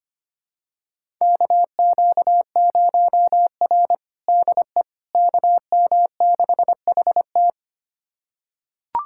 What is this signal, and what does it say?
Morse code, heard as a keyed tone, about 25 wpm, 700 Hz. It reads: KQ0R DE KM6HT
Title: USA Callsign DE Callsign 25wpm